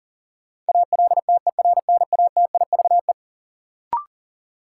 Morse code, heard as a keyed tone, about 40 wpm, 700 Hz. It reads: ALTERNATIVE
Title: Sentences from Top 1000 Words - Review Part 4 - New Word Drill - 40wpm